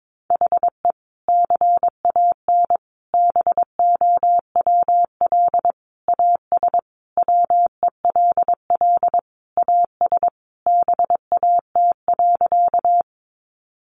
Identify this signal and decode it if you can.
Morse code, heard as a keyed tone, about 22 wpm, 700 Hz. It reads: HE CAN BOWL AS WELL AS BAT.
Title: Sentences from Top 1500 Words 22wpm